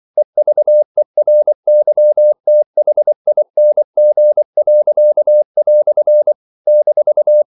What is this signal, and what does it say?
Morse code, heard as a keyed tone, about 24 wpm, 600 Hz. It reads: EVERYTHING." -
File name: CW / Practice / EnglishQuotations0324